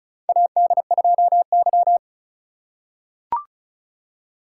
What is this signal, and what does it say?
Morse code, heard as a keyed tone, about 36 wpm, 690 Hz. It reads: AD2Y
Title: USA Call Signs - 4 Characters - 3x Character Spacing - 35wpm